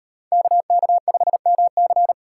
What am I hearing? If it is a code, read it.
Morse code, heard as a keyed tone, about 36 wpm, 700 Hz. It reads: KK5MC